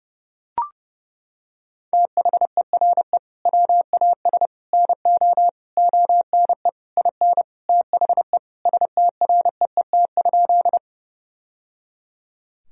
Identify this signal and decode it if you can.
Morse code, heard as a keyed tone, about 29 wpm, 700 Hz. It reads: THERE WAS NO ONE IN THE STREET?
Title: Sentences from Top 1000 Words - Review Part 3 - 30wpm